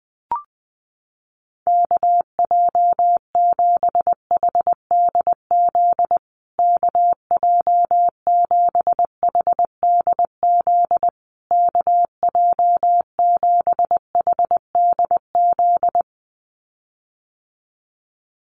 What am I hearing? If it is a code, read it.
Morse code, heard as a keyed tone, about 20 wpm, 700 Hz. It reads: KJ7HDZ KJ7HDZ KJ7HDZ
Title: USA Call Signs Repeated 3x 20wpm